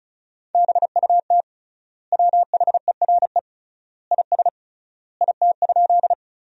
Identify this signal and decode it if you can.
Morse code, heard as a keyed tone, about 36 wpm, 700 Hz. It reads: BUT WHERE IS IT?
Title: 3x Word-Spacing - Sentences from Top 200 Words - 35wpm